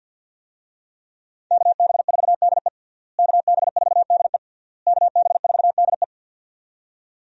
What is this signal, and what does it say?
Morse code, heard as a keyed tone, about 50 wpm, 700 Hz. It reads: KB4DE KB4DE KB4DE
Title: USA Call Signs - 5 Characters - Repeated 3x - 50wpm